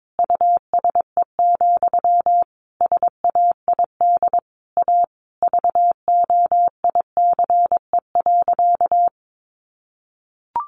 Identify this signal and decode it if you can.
Morse code, heard as a keyed tone, about 22 wpm, 700 Hz. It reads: USE, SAID A VOICE.
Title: Sentences from Top 500 Words 22wpm